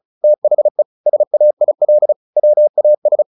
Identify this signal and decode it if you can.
Morse code, heard as a keyed tone, about 34 wpm, 600 Hz. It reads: THE SAIL WAS
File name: the old man and the sea / the old man and the sea 1 35wpm